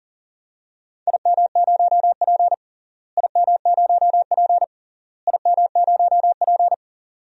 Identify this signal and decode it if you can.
Morse code, heard as a keyed tone, about 40 wpm, 700 Hz. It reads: IM0P IM0P IM0P
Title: DXCC-100 Call Signs - 4 Characters - Repeated 3x - 40wpm